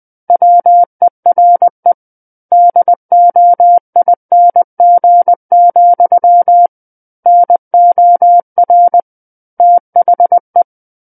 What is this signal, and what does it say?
Morse code, heard as a keyed tone, about 20 wpm, 700 Hz. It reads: WERE DOING, NOR THE
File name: 20 WPM